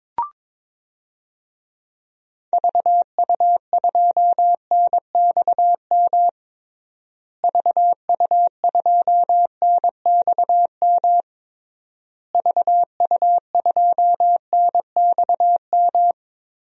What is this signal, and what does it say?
Morse code, heard as a keyed tone, about 22 wpm, 700 Hz. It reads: VU2NXM VU2NXM VU2NXM
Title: DXCC-100 Call Signs - 6 Characters - Repeated 3x - 22wpm